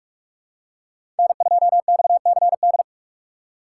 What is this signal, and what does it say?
Morse code, heard as a keyed tone, about 46 wpm, 700 Hz. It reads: N2XCD
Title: USA Call Signs - 6x Character Spacing - 45wpm